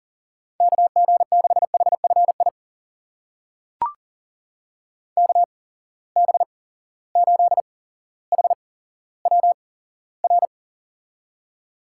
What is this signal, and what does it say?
Morse code, heard as a keyed tone, about 38 wpm, spaced as for 17 wpm, 700 Hz. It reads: KG6HFI KB8HWR
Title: USA Call Signs - 6 Characters - 8x Character Spacing - 40wpm